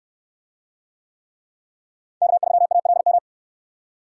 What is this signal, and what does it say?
Morse code, heard as a keyed tone, about 68 wpm, 700 Hz. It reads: K3ALM